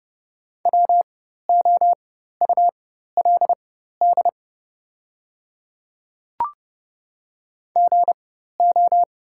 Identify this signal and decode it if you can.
Morse code, heard as a keyed tone, about 30 wpm, spaced as for 14 wpm, 700 Hz. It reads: WOULD GO
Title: 4x Character Spacing - Top 100 Words - 30wpm